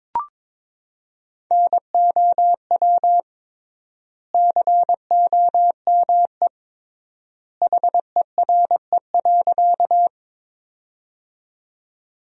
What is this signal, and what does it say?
Morse code, heard as a keyed tone, about 21 wpm, 700 Hz. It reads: NOW COME HERE.
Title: 3x Word-Spacing - Sentences from Top 200 Words - 22wpm